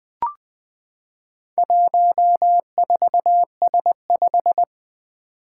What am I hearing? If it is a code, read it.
Morse code, heard as a keyed tone, about 20 wpm, 700 Hz. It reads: 14S5
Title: Four Challenging Letter-Number Combinations 20wpm